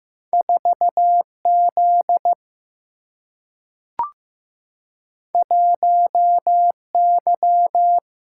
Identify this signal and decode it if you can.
Morse code, heard as a keyed tone, about 15 wpm, 700 Hz. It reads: 4Z 1Y
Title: Single-Letter Single-Number 15wpm